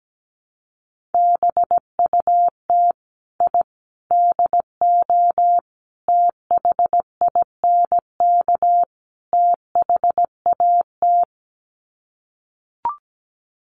Morse code, heard as a keyed tone, about 17 wpm, 700 Hz. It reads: BUT I DO THINK THAT
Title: Sets of 5 Words Encore2 17wpm